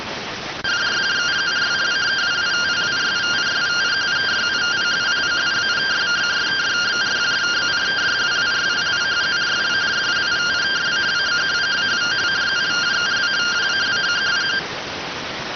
Two-tone teleprinter signal, about 170 Hz shift.